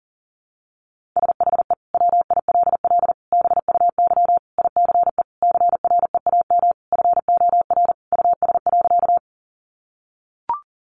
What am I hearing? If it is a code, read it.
Morse code, heard as a keyed tone, about 37 wpm, 700 Hz. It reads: SHE WILL BUY ICE CREAM FOR US.
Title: Sentences from Top 1500 Words 40wpm